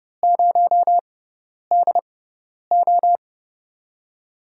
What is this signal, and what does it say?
Morse code, heard as a keyed tone, about 30 wpm, spaced as for 10 wpm, 700 Hz. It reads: 0DO
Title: USA Call Signs - 4 Characters - 6x Character Spacing - 30wpm